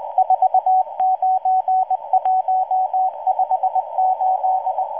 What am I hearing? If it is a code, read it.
Morse code, heard as a keyed tone, about 20 wpm, 750 Hz, with radio noise. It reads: 49158